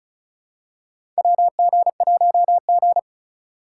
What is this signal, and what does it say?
Morse code, heard as a keyed tone, about 36 wpm, 700 Hz. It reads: WG1G
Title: USA Call Signs - 3x Character Spacing - 35wpm